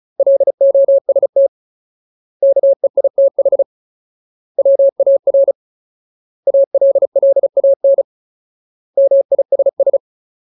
Morse code, heard as a keyed tone, about 34 wpm, 550 Hz. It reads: LOST KEITH WAR ALLAN MISS
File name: Ultimate Word List / 35WPM Ultimate